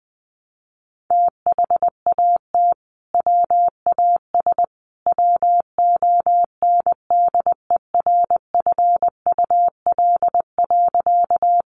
Morse code, heard as a keyed tone, about 19 wpm, 700 Hz. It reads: THAT WAS WONDERFUL.
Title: Sentences from Top 1100 Words 20wpm